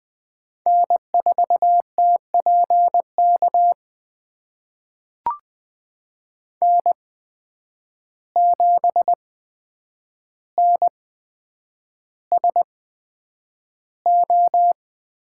Morse code, heard as a keyed tone, about 20 wpm, spaced as for 9 wpm, 700 Hz. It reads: N4TPK N7NSO
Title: USA Call Signs - 5 Characters - 8x Character Spacing - 20wpm